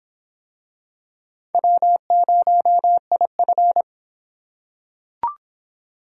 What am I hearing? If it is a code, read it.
Morse code, heard as a keyed tone, about 26 wpm, 700 Hz. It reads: W0IF